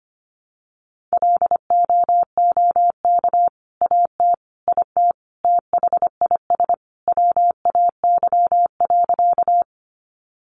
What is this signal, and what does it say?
Morse code, heard as a keyed tone, about 25 wpm, 700 Hz. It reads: LOOK AT IT THIS WAY.
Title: Sentences from Top 100 Words 25wpm